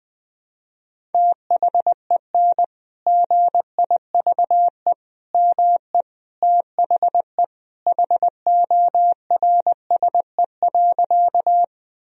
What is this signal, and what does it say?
Morse code, heard as a keyed tone, about 21 wpm, 700 Hz. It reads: THEN GIVE ME THE HORSE.
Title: Sentences from Top 1000 Words - Review Part 1 - 20wpm